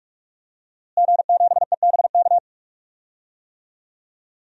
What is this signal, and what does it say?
Morse code, heard as a keyed tone, about 44 wpm, 700 Hz. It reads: G7EBK
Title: DXCC-100 Call Signs - 5 Characters - Repeated 2x - 45wpm